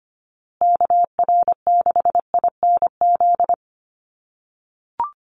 Morse code, heard as a keyed tone, about 25 wpm, 700 Hz. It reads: KR6INZ
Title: USA Call Signs Repeated 2x 25wpm